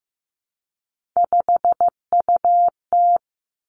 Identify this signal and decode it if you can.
Morse code, heard as a keyed tone, about 15 wpm, 700 Hz. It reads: 5UT
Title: Three Challenging Letter-Number Combinations 15wpm